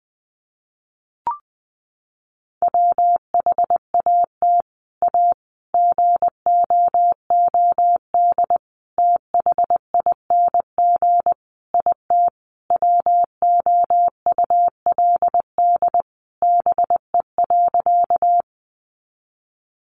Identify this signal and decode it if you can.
Morse code, heard as a keyed tone, about 20 wpm, 700 Hz. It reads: WHAT A GOOD THING IT WOULD BE.
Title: Sentences from Top 200 Words 20wpm